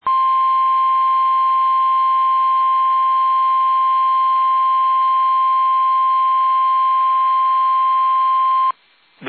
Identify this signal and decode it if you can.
Emergency Alert System (SAME) tones (the 1050 Hz attention tone). It is heard from 0.0 to 8.8 s.